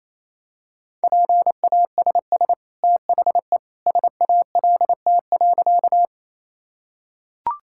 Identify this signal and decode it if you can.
Morse code, heard as a keyed tone, about 28 wpm, 700 Hz. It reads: PASS THE SALT.
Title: Sentences from Top 1000 Words - Review Part 5 - 28wpm